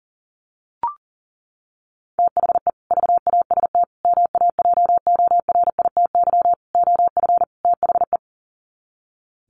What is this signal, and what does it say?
Morse code, heard as a keyed tone, about 40 wpm, 700 Hz. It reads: THE VAST MAJORITY OF THE
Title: Sets of 5 Words Encore 40wpm